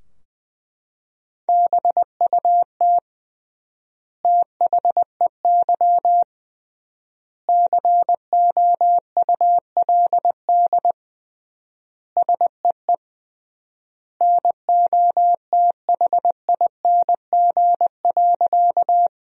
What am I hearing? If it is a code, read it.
Morse code, heard as a keyed tone, about 21 wpm, 700 Hz. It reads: BUT THEY COULD SEE NOTHING.
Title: 3x Word-Spacing - Sentences from Top 500 Words - 20wpm